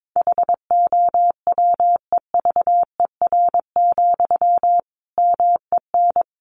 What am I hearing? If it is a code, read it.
Morse code, heard as a keyed tone, about 22 wpm, 700 Hz. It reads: HOWEVER, MEN
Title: Sentences from Top 600 Words 22wpm